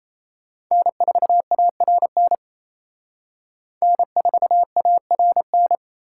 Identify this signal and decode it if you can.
Morse code, heard as a keyed tone, about 34 wpm, 700 Hz. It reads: N4ARN N4ARN
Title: Speed Racing - USA 1x3 Call Signs 28wpm